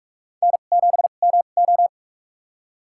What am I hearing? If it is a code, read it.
Morse code, heard as a keyed tone, about 43 wpm, spaced as for 32 wpm, 700 Hz. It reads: N7MO
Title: USA Call Signs - 4 Characters - 2x Character Spacing - 45wpm